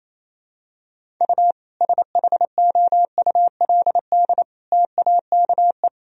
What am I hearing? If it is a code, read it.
Morse code, heard as a keyed tone, about 28 wpm, 700 Hz. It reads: U SHOULD TAKE
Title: Sentences from Top 300 Words and Abbreviations 28wpm